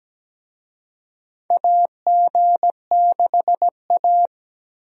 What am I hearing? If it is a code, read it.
Morse code, heard as a keyed tone, about 17 wpm, 700 Hz. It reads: AG6A
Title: USA Call Signs - 4 Characters - 17wpm